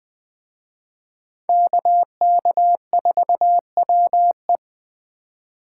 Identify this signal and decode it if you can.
Morse code, heard as a keyed tone, about 23 wpm, 700 Hz. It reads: KK4WE
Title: Speed Racing - USA 2x2 Call Signs 20wpm